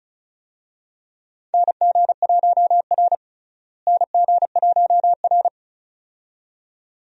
Morse code, heard as a keyed tone, about 36 wpm, 700 Hz. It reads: NG1R NG1R
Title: USA Call Signs - 4 Characters - Repeated 2x - 35wpm